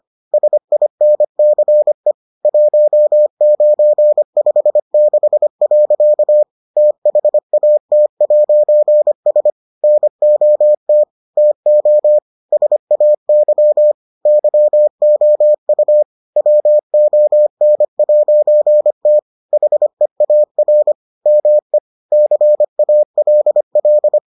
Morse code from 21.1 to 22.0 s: ME

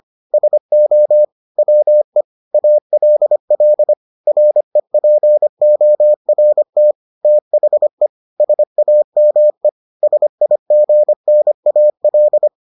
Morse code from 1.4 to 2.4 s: WE

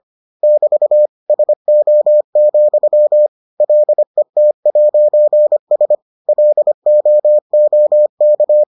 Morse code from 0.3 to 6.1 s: <BT> SO, LET'S

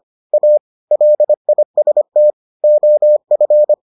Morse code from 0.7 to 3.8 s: LIST OF

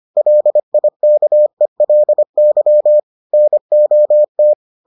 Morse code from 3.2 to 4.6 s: NOT

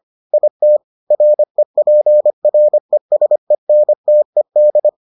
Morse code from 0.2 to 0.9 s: IT

